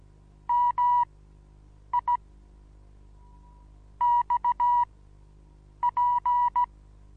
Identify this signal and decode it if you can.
Morse code, heard as a keyed tone, about 16 wpm, spaced as for 7 wpm, 970 Hz. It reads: MI X P
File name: Onlinelehrgang / morsen / DJ4UF Morsen25